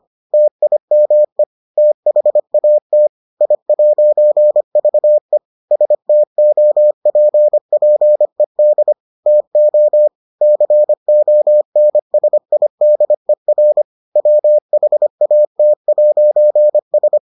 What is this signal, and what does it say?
Morse code, heard as a keyed tone, about 25 wpm, 600 Hz. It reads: TIME THAT I'VE STOPPED TO CONSIDER WHAT'S